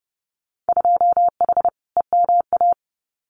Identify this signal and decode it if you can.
Morse code, heard as a keyed tone, about 30 wpm, 700 Hz. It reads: 2H EMA
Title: Winter Field Day Exchanges 30wpm